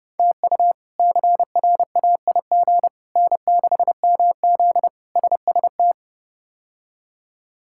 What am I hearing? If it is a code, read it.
Morse code, heard as a keyed tone, about 30 wpm, 700 Hz. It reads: TU CRAIG N6MZ SST
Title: K1USN SST QSO Practice - 30wpm